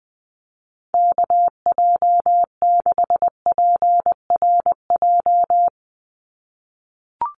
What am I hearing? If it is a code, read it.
Morse code, heard as a keyed tone, about 19 wpm, 700 Hz. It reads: KJ6PRJ
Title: USA Call Signs 20wpm